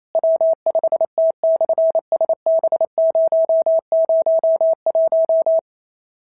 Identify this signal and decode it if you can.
Morse code, heard as a keyed tone, about 28 wpm, 650 Hz. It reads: W5T/SB001
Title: SOTA QSO Practice 28wpm